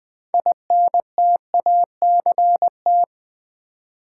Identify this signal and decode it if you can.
Morse code, heard as a keyed tone, about 20 wpm, 700 Hz. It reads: INTACT
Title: Build A Word - 2 Letters At A Time - 6-Letter Words - 20wpm